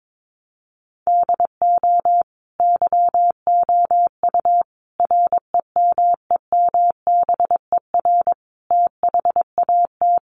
Morse code, heard as a keyed tone, about 22 wpm, 700 Hz. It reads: DO YOU REMEMBER THAT
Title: Sentences from Top 1100 Words 22wpm